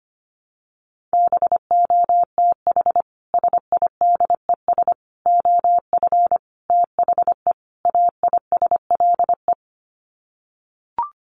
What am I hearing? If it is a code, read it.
Morse code, heard as a keyed tone, about 25 wpm, 700 Hz. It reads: BOTH SIDES OF THE AISLE
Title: Sets of 5 Words Encore2 25wpm